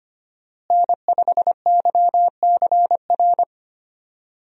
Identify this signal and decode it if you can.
Morse code, heard as a keyed tone, about 25 wpm, 700 Hz. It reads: N5YCR
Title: USA Call Signs - 5 Characters - Repeated 2x - 25wpm